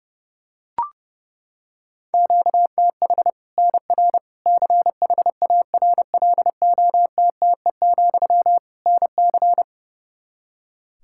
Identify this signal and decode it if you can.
Morse code, heard as a keyed tone, about 31 wpm, 700 Hz. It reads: QTH NR CHARLOTTE, NC
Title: Practice QSO - Exchange 1 - Beginning - 30wpm